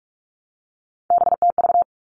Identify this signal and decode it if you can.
Morse code, heard as a keyed tone, about 44 wpm, 700 Hz. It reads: BTV
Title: All Cut Numbers - 3-Digit 45wpm